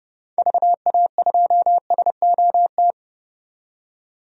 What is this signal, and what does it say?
Morse code, heard as a keyed tone, about 30 wpm, 700 Hz. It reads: VA2SOT
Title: DXCC-100 Call Signs - 6 Characters - 30wpm